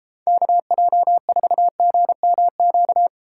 Morse code, heard as a keyed tone, about 30 wpm, 700 Hz. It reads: KJ4GMQ